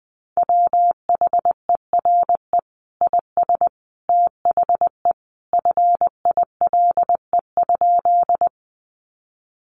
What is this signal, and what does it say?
Morse code, heard as a keyed tone, about 20 wpm, 700 Hz. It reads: WHERE IS THE FILE?
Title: Sentences from Top 1200 Words 20wpm